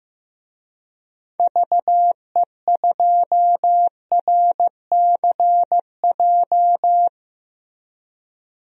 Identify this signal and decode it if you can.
Morse code, heard as a keyed tone, about 15 wpm, 700 Hz. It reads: VE2RCJ
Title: DXCC-100 Call Signs - 6 Characters - Repeated 2x - 15wpm